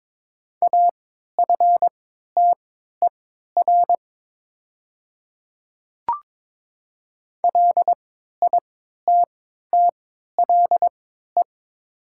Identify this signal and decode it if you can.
Morse code, heard as a keyed tone, about 22 wpm, spaced as for 13 wpm, 700 Hz. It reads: AFTER LITTLE
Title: 3x Character Spacing - Top 300 Words - 22wpm